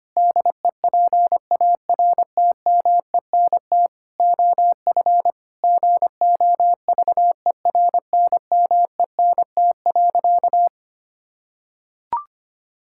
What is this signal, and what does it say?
Morse code, heard as a keyed tone, about 25 wpm, 700 Hz. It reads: DEPARTMENT OF GOVERNMENT.
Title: Sentences from Top 1000 Words - Review Part 5 - 25wpm